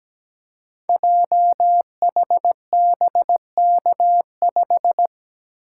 Morse code, heard as a keyed tone, about 17 wpm, 700 Hz. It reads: JHBK5
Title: Code Groups - 5 Letter:Numbers - 17wpm